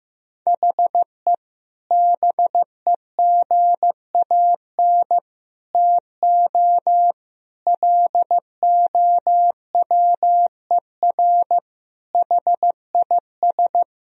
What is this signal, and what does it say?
Morse code, heard as a keyed tone, about 16 wpm, 700 Hz. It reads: HE BEGAN TO LOWER HIS
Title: Sentences from Top 1000 Words - Review Part 4 - 15wpm